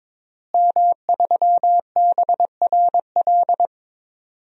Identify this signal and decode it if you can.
Morse code, heard as a keyed tone, about 22 wpm, 700 Hz. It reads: M3BRL
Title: DXCC-100 Call Signs - 5 Characters - Repeated 3x - 22wpm